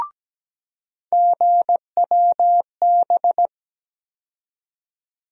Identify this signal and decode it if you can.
Morse code, heard as a keyed tone, about 17 wpm, 700 Hz. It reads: GWB
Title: USA Call Sign Postfixes - 3 Letters 17wpm